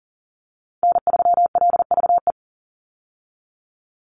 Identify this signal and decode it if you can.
Morse code, heard as a keyed tone, about 40 wpm, 700 Hz. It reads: N3LVE
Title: USA Call Signs 40wpm